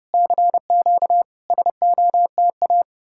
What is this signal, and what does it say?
Morse code, heard as a keyed tone, about 33 wpm, 700 Hz. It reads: CQ SOTA